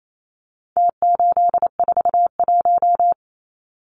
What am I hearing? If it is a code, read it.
Morse code, heard as a keyed tone, about 28 wpm, 700 Hz. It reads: T841
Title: Numbers - Contest Style - 4-Digit 28wpm